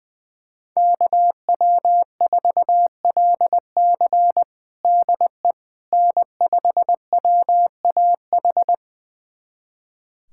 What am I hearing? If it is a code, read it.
Morse code, heard as a keyed tone, about 20 wpm, 700 Hz. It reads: KW4LC DE N5WAH
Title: USA Callsign DE Callsign - 5 Characters - 20wpm